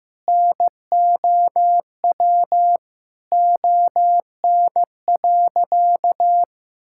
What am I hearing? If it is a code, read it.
Morse code, heard as a keyed tone, about 16 wpm, 700 Hz. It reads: NOW ON.
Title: Sentences from Top 1000 Words - Review Part 6 - 15wpm